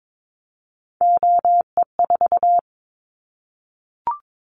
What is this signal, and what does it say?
Morse code, heard as a keyed tone, about 22 wpm, 700 Hz. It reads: OE4
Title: Callsign Prefixes for Top 25 DXCC - 22wpm